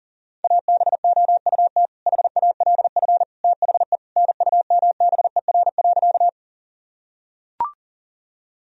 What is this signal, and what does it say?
Morse code, heard as a keyed tone, about 40 wpm, 700 Hz. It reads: ABOUT HALF THE NUMBER.
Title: Sentences from Top 1000 Words - Review Part 1 - 40wpm